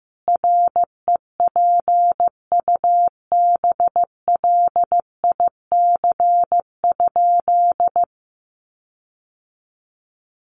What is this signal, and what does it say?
Morse code, heard as a keyed tone, about 15 wpm, 700 Hz. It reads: REPUBLIC?
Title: Sentences from Top 1500 Words 15wpm